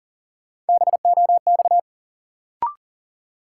Morse code, heard as a keyed tone, about 40 wpm, 700 Hz. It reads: BOX
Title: Sentences from Top 500 Words - New Word Drill - 40wpm